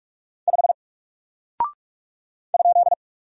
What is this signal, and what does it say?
Morse code, heard as a keyed tone, about 47 wpm, 700 Hz. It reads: 5 ?
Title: Single Character (Letter:Number:Punctuation:Prosign) Alt 45wpm